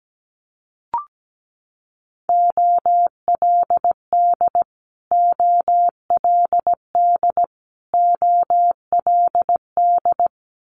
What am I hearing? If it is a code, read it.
Morse code, heard as a keyed tone, about 17 wpm, 700 Hz. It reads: OLD OLD OLD
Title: Top 200 Words Repeated 17wpm